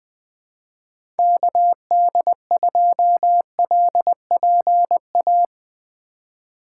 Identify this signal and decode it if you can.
Morse code, heard as a keyed tone, about 20 wpm, 700 Hz. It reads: KD2LPA
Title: USA Call Signs - 6 Characters - 3x Character Spacing - 20wpm